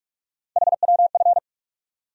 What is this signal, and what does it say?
Morse code, heard as a keyed tone, about 46 wpm, 700 Hz. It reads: HWF
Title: USA Call Sign Postfixes - 3 Letters 45wpm